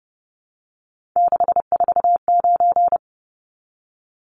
Morse code, heard as a keyed tone, about 30 wpm, 700 Hz. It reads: 649
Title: Numbers - 3-Digit 30wpm